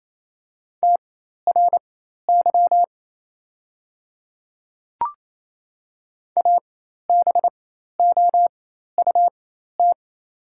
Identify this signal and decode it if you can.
Morse code, heard as a keyed tone, about 28 wpm, spaced as for 13 wpm, 690 Hz. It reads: TRY ABOUT
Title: 4x Character Spacing - Top 200 Words - 28wpm